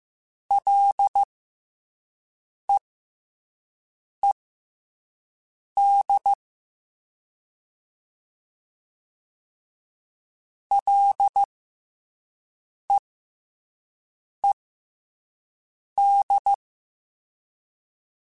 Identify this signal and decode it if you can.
Morse code, heard as a keyed tone, about 15 wpm, spaced as for 5 wpm, 800 Hz. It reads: LEED LEED
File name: F and L / F and L mp3 / F and L Prefixes 3x2 Twice @5